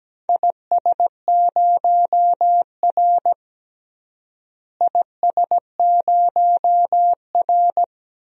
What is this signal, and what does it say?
Morse code, heard as a keyed tone, about 17 wpm, 700 Hz. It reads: IS0R IS0R
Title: DXCC-100 Call Signs - 4 Characters - Repeated 2x - 17wpm